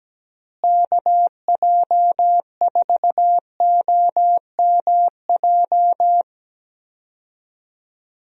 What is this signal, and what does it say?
Morse code, heard as a keyed tone, about 17 wpm, 700 Hz. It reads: KJ4OMJ
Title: USA Call Signs - 6 Characters - Repeated 2x - 17wpm